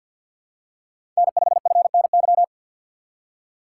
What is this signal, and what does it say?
Morse code, heard as a keyed tone, about 50 wpm, 700 Hz. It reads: N5FNY
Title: USA Call Signs - 3x Character Spacing - 50wpm